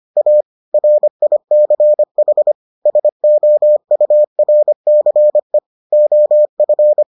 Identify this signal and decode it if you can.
Morse code, heard as a keyed tone, about 25 wpm, 600 Hz. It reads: A RICH SOURCE OF